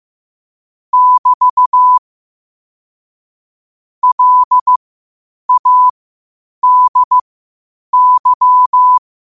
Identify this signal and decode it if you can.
Morse code, heard as a keyed tone, about 15 wpm, spaced as for 8 wpm, 1000 Hz. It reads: <BT> LADY